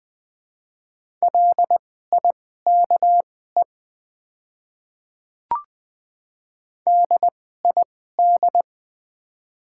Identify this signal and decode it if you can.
Morse code, heard as a keyed tone, about 20 wpm, spaced as for 15 wpm, 690 Hz. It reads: LIKE DID
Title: 2x Character Spacing - Top 100 Words - 20wpm